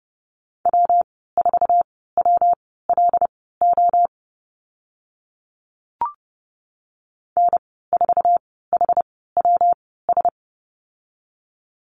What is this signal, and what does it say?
Morse code, heard as a keyed tone, about 30 wpm, spaced as for 17 wpm, 700 Hz. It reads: W4WLO N4HWS
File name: Lesson 19g - Letter w - Call Signs 30wpm